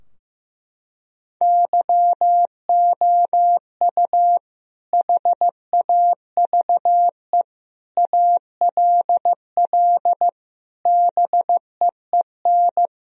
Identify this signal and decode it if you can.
Morse code, heard as a keyed tone, about 16 wpm, 700 Hz. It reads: YOU HAVE ALL BEEN
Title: Sentences from Top 1000 Words - Review Part 2 - 15wpm